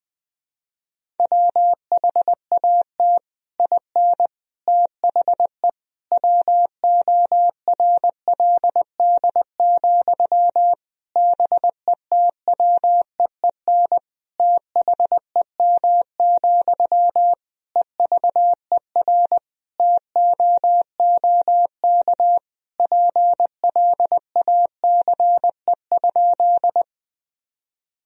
Morse code, heard as a keyed tone, about 20 wpm, 700 Hz. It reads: WHAT IN THE WORLD, BETWEEN THEM, EVER TOOK PLACE?
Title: Sentences from Top 1000 Words - Review Part 1 - 20wpm